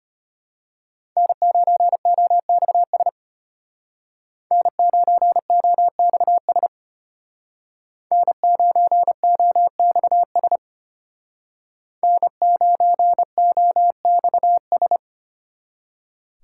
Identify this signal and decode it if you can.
Morse code, heard as a keyed tone, about 32 wpm, 700 Hz. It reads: N9OXS N9OXS N9OXS N9OXS